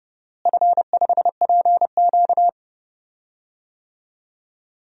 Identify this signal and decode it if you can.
Morse code, heard as a keyed tone, about 30 wpm, 700 Hz. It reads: F5PQ
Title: DXCC-100 Call Signs - 4 Characters - Repeated 2x - 30wpm